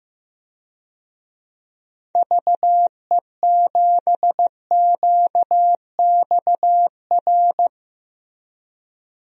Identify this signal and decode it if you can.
Morse code, heard as a keyed tone, about 15 wpm, 700 Hz. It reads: VE7QXR
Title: DXCC-100 Call Signs - 6 Characters - Repeated 2x - 15wpm